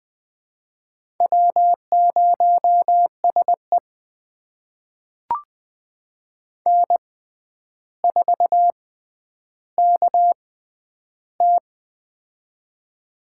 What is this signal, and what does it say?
Morse code, heard as a keyed tone, about 20 wpm, spaced as for 10 wpm, 700 Hz. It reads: W0SE N4KT
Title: USA Call Signs - 4 Characters - 6x Character Spacing - 20wpm